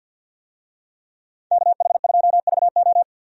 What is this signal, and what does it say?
Morse code, heard as a keyed tone, about 50 wpm, 700 Hz. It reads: KH2VO
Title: USA Call Signs - 5 Characters - 50wpm